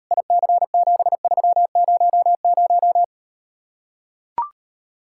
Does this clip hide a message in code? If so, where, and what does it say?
Yes: Morse code from 0.1 to 3.0 s, heard as a keyed tone, about 38 wpm, 700 Hz. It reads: IC7300